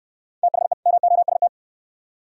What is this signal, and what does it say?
Morse code, heard as a keyed tone, about 61 wpm, 700 Hz. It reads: THE MOST